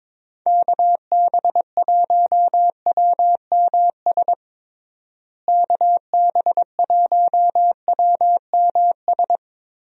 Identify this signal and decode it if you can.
Morse code, heard as a keyed tone, about 22 wpm, 700 Hz. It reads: KB1WMS KB1WMS
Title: USA Call Signs - 6 Characters - Repeated 2x - 22wpm